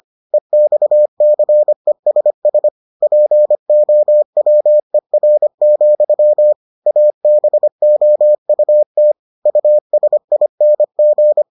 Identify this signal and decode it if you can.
Morse code, heard as a keyed tone, about 25 wpm, 600 Hz. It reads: EXCESS POWER, ABOUT USING